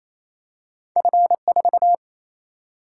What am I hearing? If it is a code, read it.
Morse code, heard as a keyed tone, about 28 wpm, 700 Hz. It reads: F4
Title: Single-Letter Single-Number 28wpm